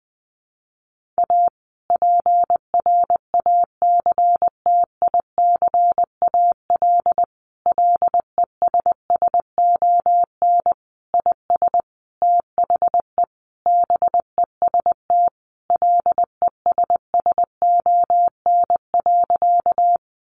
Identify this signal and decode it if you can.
Morse code, heard as a keyed tone, about 19 wpm, 700 Hz. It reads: A PRACTICAL LESSON IS THE BEST LESSON.
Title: Sentences from Top 1400 Words 20wpm